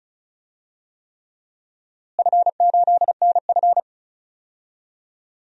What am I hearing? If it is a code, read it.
Morse code, heard as a keyed tone, about 36 wpm, 700 Hz. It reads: F8NF
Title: DXCC-100 Call Signs - 4 Characters - Repeated 3x - 35wpm